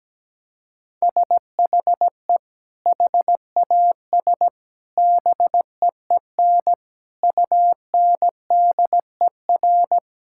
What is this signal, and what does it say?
Morse code, heard as a keyed tone, about 17 wpm, 700 Hz. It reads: SHE HAS BEEN UNDER
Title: Sentences from Top 1000 Words - Review Part 6 - 17wpm